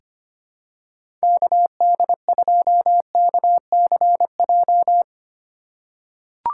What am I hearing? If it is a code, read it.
Morse code, heard as a keyed tone, about 25 wpm, 700 Hz. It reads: KD2KCJ
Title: USA Call Signs - 6 Characters - 3x Character Spacing - 25wpm